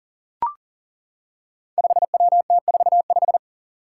Morse code, heard as a keyed tone, about 39 wpm, 700 Hz. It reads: 5WT45
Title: Five Challenging Letter-Number Combinations 40wpm